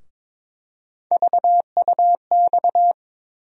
Morse code, heard as a keyed tone, about 22 wpm, 700 Hz. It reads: VUX
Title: USA Call Sign Postfixes - 3 Letters 22wpm